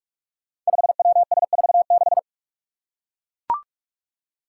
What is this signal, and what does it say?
Morse code, heard as a keyed tone, about 41 wpm, 700 Hz. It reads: 5WS46
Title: Five Challenging Letter-Number Combinations 45wpm